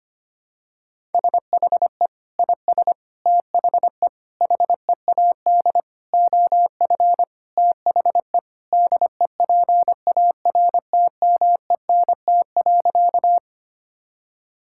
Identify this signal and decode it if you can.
Morse code, heard as a keyed tone, about 25 wpm, 700 Hz. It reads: SHE IS THE HEAD OF THE DEPARTMENT.
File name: practice / 25wpm / Sentences with 3-Word Phrases - 25wpm